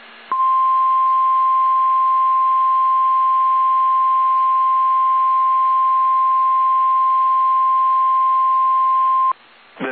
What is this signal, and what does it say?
Emergency Alert System (SAME) tones (the 1050 Hz attention tone). It is heard from 0.3 to 9.3 s.
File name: eas / KSDZKDJL / KSDZKDJL